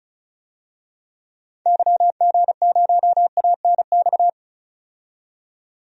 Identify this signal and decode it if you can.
Morse code, heard as a keyed tone, about 36 wpm, 700 Hz. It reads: YG0ANX
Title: DXCC-100 Call Signs - 6 Characters - 35wpm